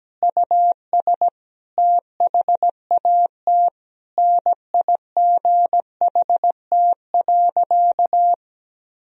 Morse code, heard as a keyed tone, about 17 wpm, 700 Hz. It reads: US THAT NIGHT.
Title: Sentences from Top 1000 Words - Review Part 2 - 17wpm